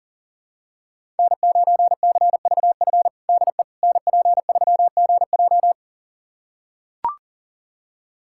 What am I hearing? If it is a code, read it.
Morse code, heard as a keyed tone, about 40 wpm, 700 Hz. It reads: N9CVF DE NP3GJ
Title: USA Callsign DE Callsign - 5 Characters - 40wpm